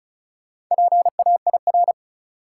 Morse code, heard as a keyed tone, about 36 wpm, 700 Hz. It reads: PAIR
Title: Four Letter Words - 35wpm